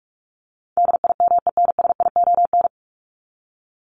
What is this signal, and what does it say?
Morse code, heard as a keyed tone, about 44 wpm, 700 Hz. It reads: DIMENSION
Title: Top 1501-1750 Words 45wpm